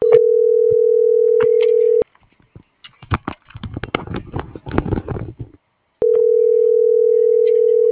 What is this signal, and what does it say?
Ringback tone (North America), 440 + 480 Hz.